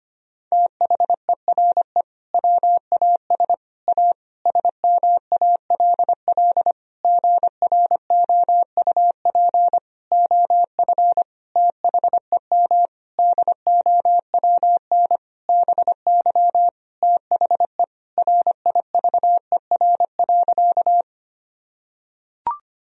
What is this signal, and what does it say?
Morse code, heard as a keyed tone, about 25 wpm, 700 Hz. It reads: THERE WAS A SMALL GROUP OF THEM DOWN BY THE RIVER.
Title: Sentences from Top 1000 Words - Review Part 1 - 25wpm